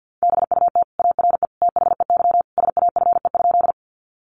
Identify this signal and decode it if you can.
Morse code, heard as a keyed tone, about 50 wpm, 700 Hz. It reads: BUT ARE THEY SAFE?
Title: Sentences from Top 900 Words 50wpm